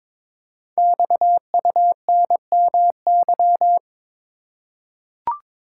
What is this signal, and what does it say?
Morse code, heard as a keyed tone, about 22 wpm, 700 Hz. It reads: XUNMY
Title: Code Groups - 5 Letters - 22wpm